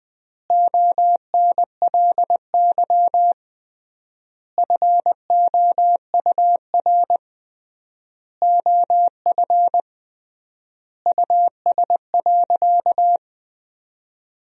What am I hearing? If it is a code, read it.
Morse code, heard as a keyed tone, about 21 wpm, 700 Hz. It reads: ONLY FOUR OF US.
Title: 3x Word-Spacing - Sentences from Top 300 Words - 20wpm